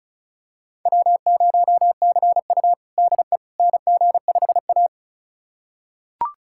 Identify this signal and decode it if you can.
Morse code, heard as a keyed tone, about 36 wpm, 700 Hz. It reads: W0CU DE NG5A
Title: USA Callsign DE Callsign - 4 Characters - 35wpm